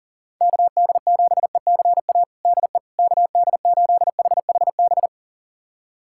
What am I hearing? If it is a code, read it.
Morse code, heard as a keyed tone, about 40 wpm, 700 Hz. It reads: KD7ECA DE KD8HHB
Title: USA Callsign DE Callsign - 6 Characters - 40wpm